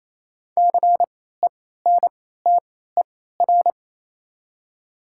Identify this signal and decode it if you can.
Morse code, heard as a keyed tone, about 27 wpm, spaced as for 16 wpm, 700 Hz. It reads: CENTER
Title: 3x Character Spacing - Top 500 Words - 28wpm